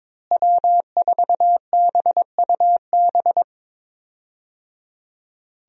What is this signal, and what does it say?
Morse code, heard as a keyed tone, about 22 wpm, 700 Hz. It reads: W4BUB
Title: USA Call Signs - 5 Characters - Repeated 3x - 22wpm